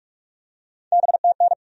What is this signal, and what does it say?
Morse code, heard as a keyed tone, about 42 wpm, 700 Hz. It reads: BTN
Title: USA Call Sign Postfixes - 3 Letters 45wpm